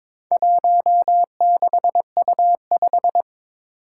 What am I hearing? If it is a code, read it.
Morse code, heard as a keyed tone, about 22 wpm, 700 Hz. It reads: 16U5
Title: Four Challenging Letter-Number Combinations 22wpm